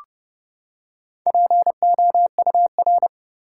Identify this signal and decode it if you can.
Morse code, heard as a keyed tone, about 30 wpm, 700 Hz. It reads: POUR
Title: Four Letter Words - 30wpm